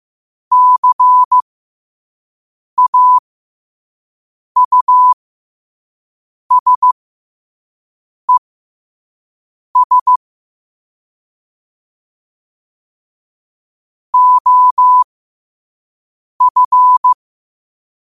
Morse code, heard as a keyed tone, about 15 wpm, spaced as for 5 wpm, 1000 Hz. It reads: CAUSES OF
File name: qotdfiles / 5wpm